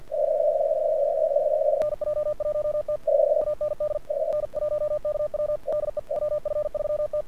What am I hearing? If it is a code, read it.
Morse code, heard as a keyed tone, about 49 wpm, 1200 Hz, with radio noise. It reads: NJ1T AND N1KW DE WF3T
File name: archived / CRSnet